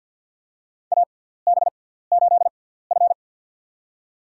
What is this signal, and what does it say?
Morse code, heard as a keyed tone, about 47 wpm, spaced as for 17 wpm, 700 Hz. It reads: AB8F
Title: USA Call Signs - 4 Characters - 6x Character Spacing - 50wpm